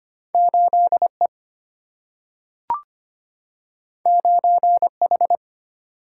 Morse code, heard as a keyed tone, about 25 wpm, 700 Hz. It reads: 8 E 9 H
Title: Single-Letter Single-Number 25wpm